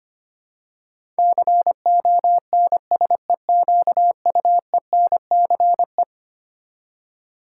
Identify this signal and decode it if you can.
Morse code, heard as a keyed tone, about 25 wpm, 700 Hz. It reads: CONSEQUENCE
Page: morse-code-ninja-practice-25wpm